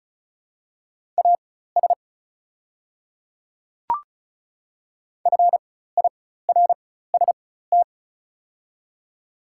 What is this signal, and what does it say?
Morse code, heard as a keyed tone, about 34 wpm, spaced as for 16 wpm, 690 Hz. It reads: AS FIRST
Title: 4x Character Spacing - Top 100 Words - 35wpm